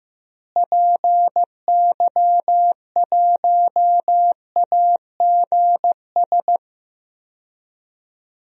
Morse code, heard as a keyed tone, about 15 wpm, 700 Hz. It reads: PY1AGS
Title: DXCC-100 Call Signs - 6 Characters - Repeated 2x - 15wpm